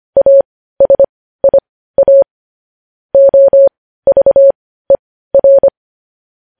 Morse code, heard as a keyed tone, about 25 wpm, spaced as for 15 wpm, 550 Hz. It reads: ASIA OVER
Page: News Headlines in Morse Code at 15 WPM